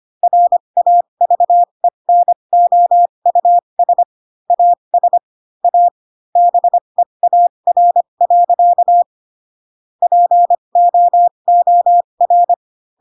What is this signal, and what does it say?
Morse code, heard as a keyed tone, about 25 wpm, spaced as for 22 wpm, 700 Hz. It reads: RAVENOUS AS A BEAR. POOR